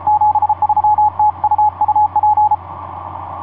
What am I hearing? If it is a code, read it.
Morse code, heard as a keyed tone, about 33 wpm, 850 Hz. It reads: 73 TU UP
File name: xe2ww f5vh etc 12m cw 18feb2010 1644